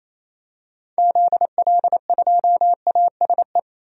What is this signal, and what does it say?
Morse code, heard as a keyed tone, about 28 wpm, 700 Hz. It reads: ZL2ASE
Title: DXCC-100 Call Signs - 6 Characters - 28wpm